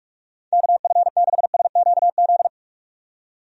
Morse code, heard as a keyed tone, about 46 wpm, 700 Hz. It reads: KF6SQZ
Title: USA Call Signs - 6 Characters - 45wpm